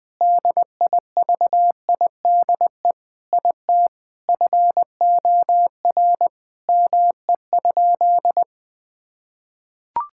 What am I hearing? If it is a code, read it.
Morse code, heard as a keyed tone, about 20 wpm, 700 Hz. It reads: DIVIDE IT FOR ME?
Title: Sentences from Top 1600 Words 20wpm